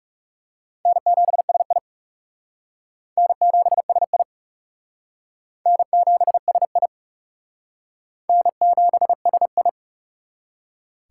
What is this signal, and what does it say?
Morse code, heard as a keyed tone, about 40 wpm, 700 Hz. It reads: N7SI N7SI N7SI N7SI